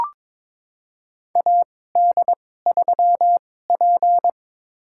Morse code, heard as a keyed tone, about 22 wpm, spaced as for 16 wpm, 700 Hz. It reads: AD3P
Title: USA Call Signs - 2x Character Spacing - 22wpm